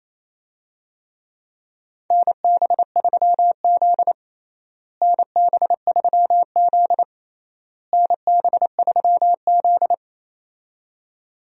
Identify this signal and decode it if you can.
Morse code, heard as a keyed tone, about 28 wpm, 700 Hz. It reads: NB3Z NB3Z NB3Z
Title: USA Call Signs - 4 Characters - Repeated 3x - 28wpm